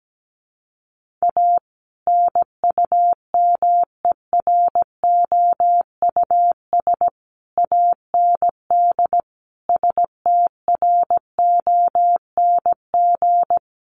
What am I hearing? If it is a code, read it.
Morse code, heard as a keyed tone, about 17 wpm, 700 Hz. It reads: A NUMEROUS AND STRONG
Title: Sentences from Top 1400 Words 17wpm